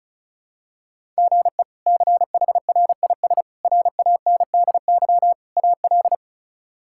Morse code, heard as a keyed tone, about 34 wpm, 700 Hz. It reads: GE CHRIS RANDY AL
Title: K1USN SST QSO Practice - 35wpm